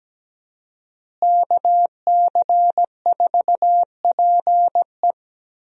Morse code, heard as a keyed tone, about 17 wpm, 700 Hz. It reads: KC4PE
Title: USA Call Signs - 5 Characters - Repeated 2x - 17wpm